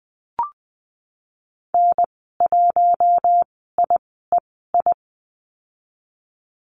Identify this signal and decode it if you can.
Morse code, heard as a keyed tone, about 19 wpm, spaced as for 14 wpm, 700 Hz. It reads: N1IEI
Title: 076 Lesson 13g - Letter L - Call Signs 20wpm